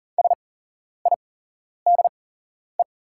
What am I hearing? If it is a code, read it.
Morse code, heard as a keyed tone, about 40 wpm, spaced as for 11 wpm, 690 Hz. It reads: SIDE
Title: 8x Character Spacing - Top 100 Words - 40wpm